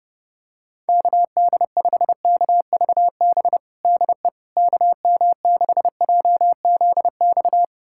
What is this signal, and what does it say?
Morse code, heard as a keyed tone, about 32 wpm, 700 Hz. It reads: KD5KVB DE KM6JZX
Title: USA Callsign DE Callsign - 6 Characters - 30wpm